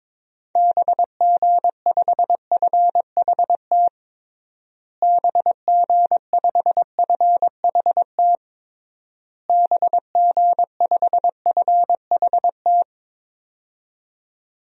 Morse code, heard as a keyed tone, about 21 wpm, 700 Hz. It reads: BG5FHT BG5FHT BG5FHT
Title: DXCC-100 Call Signs - 6 Characters - Repeated 3x - 22wpm